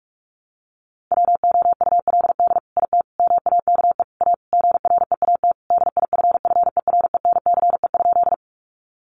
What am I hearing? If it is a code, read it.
Morse code, heard as a keyed tone, about 44 wpm, 700 Hz. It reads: WOULD IT MAKE A GREAT DIFFERENCE?
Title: Sentences from Top 800 Words 45wpm